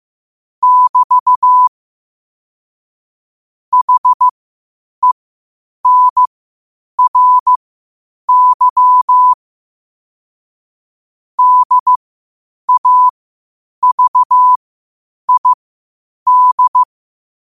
Morse code, heard as a keyed tone, about 15 wpm, spaced as for 8 wpm, 1000 Hz. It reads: <BT> HENRY DAVID
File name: qotdfiles / 8wpm